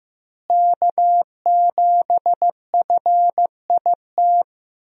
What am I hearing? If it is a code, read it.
Morse code, heard as a keyed tone, about 15 wpm, 700 Hz. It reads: K7FIT
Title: USA Call Signs - 5 Characters - Repeated 3x - 15wpm